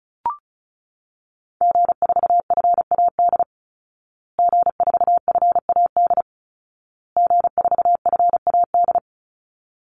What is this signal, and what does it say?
Morse code, heard as a keyed tone, about 34 wpm, 700 Hz. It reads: G4FAD G4FAD G4FAD
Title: Call Signs Repeated 2x - 35wpm